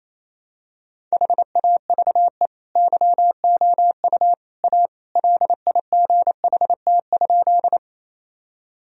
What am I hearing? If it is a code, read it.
Morse code, heard as a keyed tone, about 28 wpm, 700 Hz. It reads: HAVE YOU A LIGHT?
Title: Sentences from Top 1000 Words - Review Part 1 - 28wpm